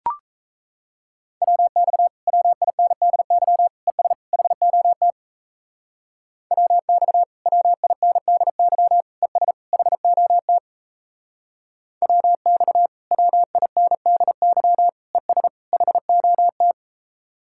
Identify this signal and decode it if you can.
Morse code, heard as a keyed tone, about 36 wpm, 700 Hz. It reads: WX WINDY ES HOT WX WINDY ES HOT WX WINDY ES HOT